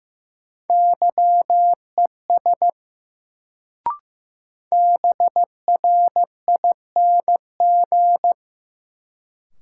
Morse code, heard as a keyed tone, about 15 wpm, 700 Hz. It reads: YES BRING
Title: Sentences from Top 500 Words - New Word Drill - 15wpm